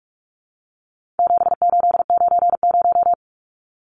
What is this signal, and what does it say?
Morse code, heard as a keyed tone, about 44 wpm, 700 Hz. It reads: 7890
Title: Numbers - 4-Digit 45wpm